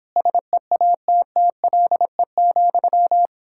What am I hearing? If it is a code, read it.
Morse code, heard as a keyed tone, about 31 wpm, 700 Hz. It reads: SEATTLE,